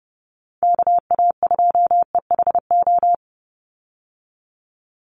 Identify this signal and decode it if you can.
Morse code, heard as a keyed tone, about 30 wpm, 700 Hz. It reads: KA2EHO
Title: USA Call Signs 30wpm